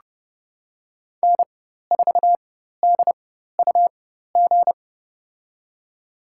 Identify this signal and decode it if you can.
Morse code, heard as a keyed tone, about 30 wpm, spaced as for 14 wpm, 700 Hz. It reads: N4DUG
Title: USA Call Signs - 5 Characters - 4x Character Spacing - 30wpm